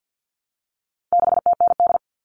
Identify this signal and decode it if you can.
Morse code, heard as a keyed tone, about 50 wpm, 700 Hz. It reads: 6TND
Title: All Cut Numbers - 4-Digit 50wpm